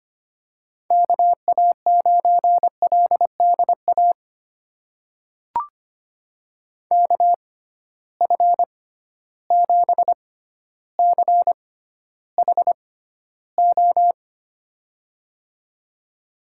Morse code, heard as a keyed tone, about 25 wpm, spaced as for 13 wpm, 700 Hz. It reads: KA9LDA KF7CHO
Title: USA Call Signs - 6 Characters - 6x Character Spacing - 25wpm